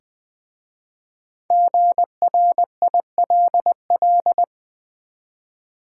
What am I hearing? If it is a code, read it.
Morse code, heard as a keyed tone, about 20 wpm, 700 Hz. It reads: GRILL
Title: Easy Words with Hints - 20wpm